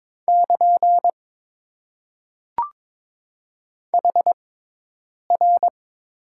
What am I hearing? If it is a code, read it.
Morse code, heard as a keyed tone, about 22 wpm, spaced as for 8 wpm, 700 Hz. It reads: <KN> HR
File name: Top 90 QSO Elements - 6x Character Spacing - 22wpm